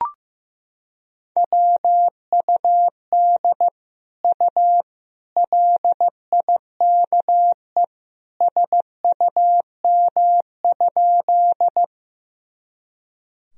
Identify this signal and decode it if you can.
Morse code, heard as a keyed tone, about 15 wpm, 700 Hz. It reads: WUD U LIKE SUM?
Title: Sentences from Top 100 Words and Abbreviations 15wpm